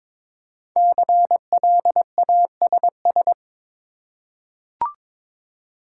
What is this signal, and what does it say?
Morse code, heard as a keyed tone, about 22 wpm, 700 Hz. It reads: CLASS
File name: 5 Letter Words - First 3 Letters Given - 22wpm